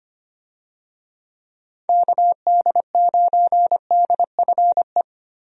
Morse code, heard as a keyed tone, about 25 wpm, 700 Hz. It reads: KD9DFE
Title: USA Call Signs - 6 Characters - Repeated 2x - 25wpm